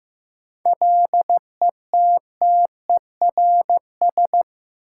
Morse code, heard as a keyed tone, about 15 wpm, 700 Hz. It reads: LETTERS
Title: Sentences from Top 1000 Words - Review Part 6 - New Word Drill - 15wpm